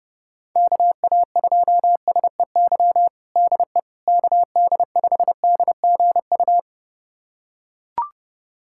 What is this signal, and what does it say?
Morse code, heard as a keyed tone, about 30 wpm, 700 Hz. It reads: KA2SEY DE KD5DGU
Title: USA Callsign DE Callsign 30wpm